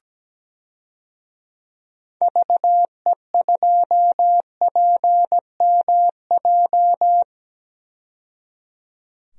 Morse code, heard as a keyed tone, about 17 wpm, 700 Hz. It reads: VE2PMJ
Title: DXCC-100 Call Signs - 6 Characters - 17wpm